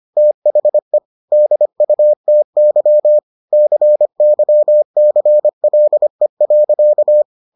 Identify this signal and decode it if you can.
Morse code, heard as a keyed tone, about 25 wpm, 600 Hz. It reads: THE DUTY CYCLE.